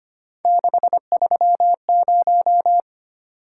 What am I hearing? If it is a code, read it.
Morse code, heard as a keyed tone, about 25 wpm, 700 Hz. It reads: 630
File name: Three Letter-Number Combinations - 25wpm